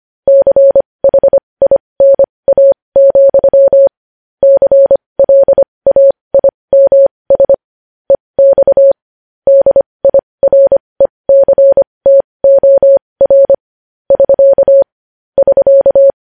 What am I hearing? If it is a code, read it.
Morse code, heard as a keyed tone, about 25 wpm, spaced as for 20 wpm, 550 Hz. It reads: CHINA, CLAIMS EX DIRECTOR <SK> <SK>
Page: News Headlines in Morse Code at 20 WPM